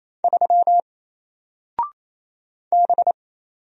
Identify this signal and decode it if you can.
Morse code, heard as a keyed tone, about 27 wpm, 700 Hz. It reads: 3 B
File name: Single-Letter-Number-28wpm